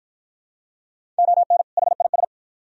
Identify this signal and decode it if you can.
Morse code, heard as a keyed tone, about 53 wpm, 700 Hz. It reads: ON HIS